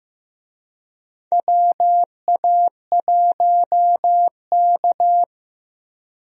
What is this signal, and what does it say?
Morse code, heard as a keyed tone, about 15 wpm, 700 Hz. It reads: WA1K
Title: USA Call Signs - 4 Characters - Repeated 3x - 15wpm